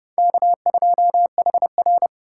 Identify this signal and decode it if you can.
Morse code, heard as a keyed tone, about 27 wpm, 700 Hz. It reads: K2HR